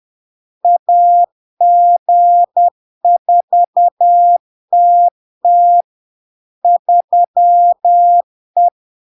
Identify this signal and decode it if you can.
Morse code, heard as a keyed tone, about 10 wpm, 700 Hz. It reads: AG4TT 3E
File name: LARK-FD-2022-Log-in-CW-10wpm-700-Hz